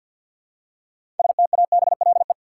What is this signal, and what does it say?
Morse code, heard as a keyed tone, about 52 wpm, 700 Hz. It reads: STABLE
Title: Six Letter Words - 50wpm